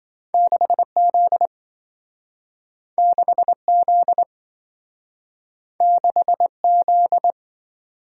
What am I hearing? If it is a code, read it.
Morse code, heard as a keyed tone, about 27 wpm, 700 Hz. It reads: 6Z 6Z 6Z